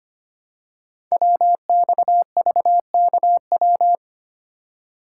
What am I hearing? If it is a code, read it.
Morse code, heard as a keyed tone, about 25 wpm, 700 Hz. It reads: WXVKW
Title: Code Groups - 5 Letters - 25wpm